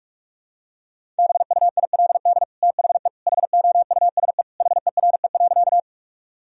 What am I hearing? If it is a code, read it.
Morse code, heard as a keyed tone, about 41 wpm, 700 Hz. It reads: BUILD THE HOUSE HERE.
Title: Sentences from Top 1000 Words - Review Part 1 - 45wpm